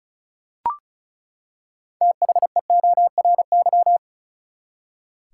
Morse code, heard as a keyed tone, about 36 wpm, 700 Hz. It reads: THEORY
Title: Six Letter Words - 35wpm